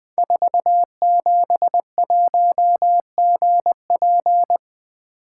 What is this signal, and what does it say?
Morse code, heard as a keyed tone, about 20 wpm, 700 Hz. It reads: 471GP
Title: Code Groups - 5 Letter:Numbers - 20wpm